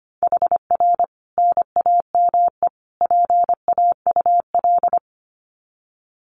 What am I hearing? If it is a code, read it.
Morse code, heard as a keyed tone, about 25 wpm, 700 Hz. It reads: HR NAME PAUL
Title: Common QSO Phrases 25wpm